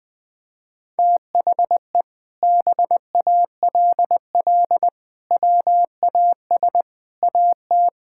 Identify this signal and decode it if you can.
Morse code, heard as a keyed tone, about 20 wpm, 700 Hz. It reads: THE BALL WAS AT
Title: Sentences from Top 1600 Words 20wpm